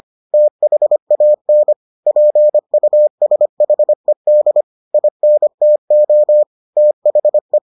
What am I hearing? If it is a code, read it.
Morse code, heard as a keyed tone, about 25 wpm, 600 Hz. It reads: THAN PUSHED INTO THE